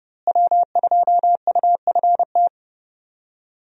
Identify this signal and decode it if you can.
Morse code, heard as a keyed tone, about 30 wpm, 700 Hz. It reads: W2UFT
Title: USA Call Signs - 5 Characters - Repeated 3x - 30wpm